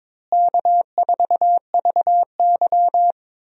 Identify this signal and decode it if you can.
Morse code, heard as a keyed tone, about 22 wpm, 700 Hz. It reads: K4VY
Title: USA Call Signs - 4 Characters - Repeated 2x - 22wpm